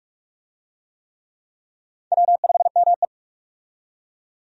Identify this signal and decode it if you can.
Morse code, heard as a keyed tone, about 44 wpm, 700 Hz. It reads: W5ME
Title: USA Call Signs - 4 Characters - 45wpm